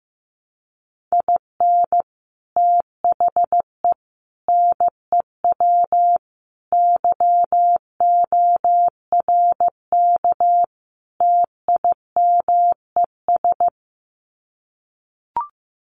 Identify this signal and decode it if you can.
Morse code, heard as a keyed tone, about 15 wpm, 700 Hz. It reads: IN THE NEW YORK TIMES
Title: Sets of 5 Words 15wpm